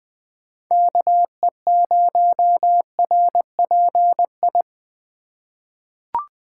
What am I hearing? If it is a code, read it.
Morse code, heard as a keyed tone, about 20 wpm, 700 Hz. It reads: KE0RPI
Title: USA Call Signs - 6 Characters - 2x Character Spacing - 20wpm